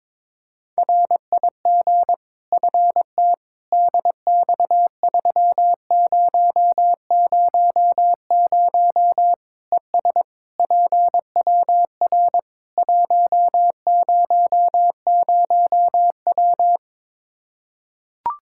Morse code, heard as a keyed tone, about 21 wpm, 700 Hz. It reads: RIG FT DX3000 ES PWR 100W
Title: Practice QSO - Exchange 2 - Info - 22wpm